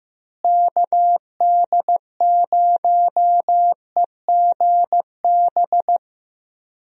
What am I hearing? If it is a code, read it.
Morse code, heard as a keyed tone, about 15 wpm, 700 Hz. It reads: KD0EGB
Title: USA Call Signs - 6 Characters - Repeated 2x - 15wpm